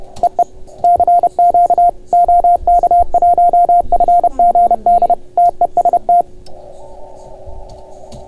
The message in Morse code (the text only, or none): EE CQ OK1FGD TEST